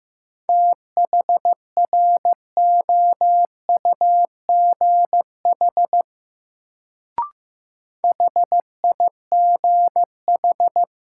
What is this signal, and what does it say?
Morse code, heard as a keyed tone, about 15 wpm, 700 Hz. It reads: THROUGH HIGH
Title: Sentences from Top 200 Words - New Word Drill - 15wpm